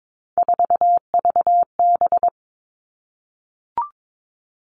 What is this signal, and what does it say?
Morse code, heard as a keyed tone, about 22 wpm, 700 Hz. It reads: 4VB
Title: Three Challenging Letter-Number Combinations 22wpm